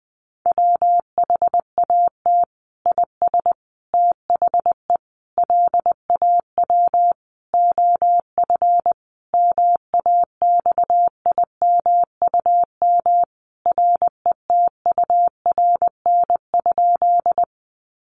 Morse code, heard as a keyed tone, about 20 wpm, 700 Hz. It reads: WHAT IS THE LAW OF MAXIMUM RETURN?
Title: Sentences from Top 1300 Words 20wpm